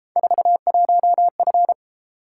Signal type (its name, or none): Morse code